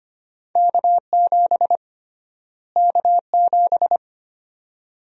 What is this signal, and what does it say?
Morse code, heard as a keyed tone, about 25 wpm, 700 Hz. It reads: K7 K7
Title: USA Call Sign Prefixes - Repeated 2x - 25wpm